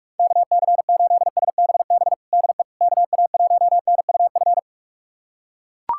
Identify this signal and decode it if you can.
Morse code, heard as a keyed tone, about 41 wpm, 700 Hz. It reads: KC8SBB DE KA1NUF
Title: USA Callsign DE Callsign - 6 Characters - 45wpm